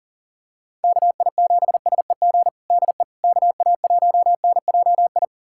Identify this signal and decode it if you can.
Morse code, heard as a keyed tone, about 40 wpm, 700 Hz. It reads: KI7SEG DE KA1NJI
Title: USA Callsign DE Callsign - 6 Characters - 40wpm